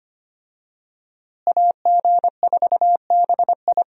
Morse code, heard as a keyed tone, about 25 wpm, 700 Hz. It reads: AG4BI
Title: USA Call Signs - 5 Characters - Repeated 3x - 25wpm